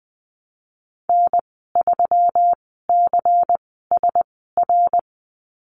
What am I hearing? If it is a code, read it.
Morse code, heard as a keyed tone, about 20 wpm, spaced as for 15 wpm, 700 Hz. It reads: N3CSR
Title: 147 Lesson 24g - Letter y - Call Signs 20wpm